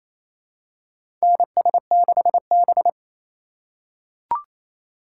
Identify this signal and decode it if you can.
Morse code, heard as a keyed tone, about 27 wpm, 700 Hz. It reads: NS6B
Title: USA Call Signs - 4 Characters - 3x Character Spacing - 28wpm